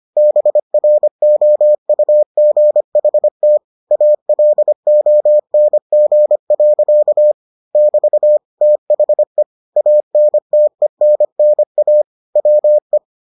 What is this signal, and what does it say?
Morse code, heard as a keyed tone, about 25 wpm, 600 Hz. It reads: BROUGHT ALONG. <BT> THE ANTENNA WE